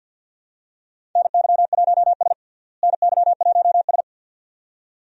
Morse code, heard as a keyed tone, about 48 wpm, 690 Hz. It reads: NY1S NY1S
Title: USA Call Signs - 4 Characters - Repeated 2x - 50wpm